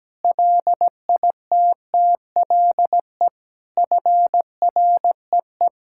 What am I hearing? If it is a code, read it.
Morse code, heard as a keyed tone, about 17 wpm, 700 Hz. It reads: LITTLE FREE
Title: Sentences from Top 1000 Words - Review Part 5 - 17wpm